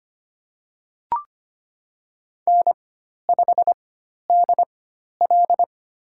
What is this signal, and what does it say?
Morse code, heard as a keyed tone, about 25 wpm, spaced as for 12 wpm, 700 Hz. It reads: N5DL
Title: USA Call Signs - 4 Characters - 4x Character Spacing - 25wpm